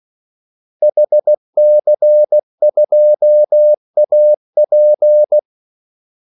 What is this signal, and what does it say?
Morse code, heard as a keyed tone, about 16 wpm, 600 Hz. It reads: HC2AP
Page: Kaldesignaler 16 wpm